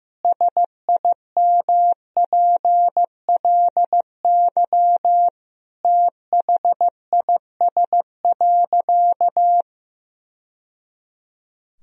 Morse code, heard as a keyed tone, about 16 wpm, 700 Hz. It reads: SIMPLY THIS.
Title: Sentences from Top 1000 Words - Review Part 4 - 15wpm